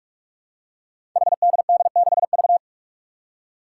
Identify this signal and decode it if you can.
Morse code, heard as a keyed tone, about 44 wpm, 700 Hz. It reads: HDD6V
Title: Five Challenging Letter-Number Combinations 45wpm